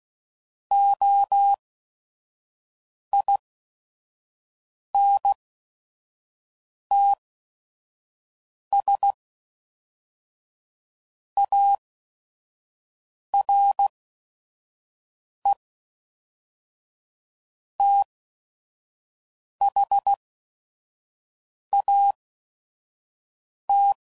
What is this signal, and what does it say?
Morse code, heard as a keyed tone, about 16 wpm, 800 Hz. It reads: O I N T S A R E T H A T